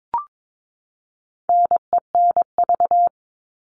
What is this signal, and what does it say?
Morse code, heard as a keyed tone, about 22 wpm, 700 Hz. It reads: NENV
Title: All Cut Numbers - 4-Digit 22wpm